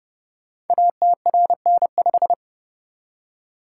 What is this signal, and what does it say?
Morse code, heard as a keyed tone, about 30 wpm, 700 Hz. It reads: ATRN5
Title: Code Groups - 5 Letter:Numbers - 30wpm